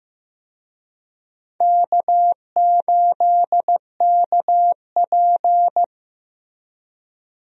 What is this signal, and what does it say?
Morse code, heard as a keyed tone, about 15 wpm, 700 Hz. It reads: K8KP
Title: USA Call Signs - 4 Characters - Repeated 3x - 15wpm